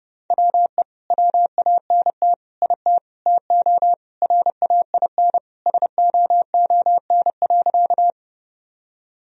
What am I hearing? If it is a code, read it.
Morse code, heard as a keyed tone, about 32 wpm, 700 Hz. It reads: WE WANT IT TO RAIN SOON.
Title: Sentences from Top 1000 Words - Review Part 1 - 30wpm